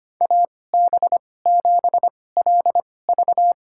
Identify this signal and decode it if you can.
Morse code, heard as a keyed tone, about 25 wpm, spaced as for 18 wpm, 700 Hz. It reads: AB7LV
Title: USA Call Signs - 5 Characters - 2x Character Spacing - 25wpm